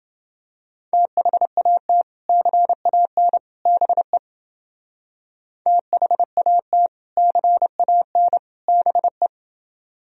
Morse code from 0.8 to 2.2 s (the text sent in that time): THAT